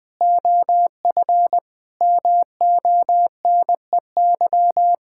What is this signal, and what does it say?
Morse code, heard as a keyed tone, about 20 wpm, 700 Hz. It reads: OF MONEY
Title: Sentences from Top 1000 Words - Review Part 6 - 20wpm